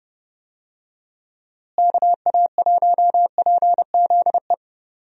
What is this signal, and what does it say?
Morse code, heard as a keyed tone, about 32 wpm, 700 Hz. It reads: KA1PZE
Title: USA Call Signs - 6 Characters - Repeated 3x - 30wpm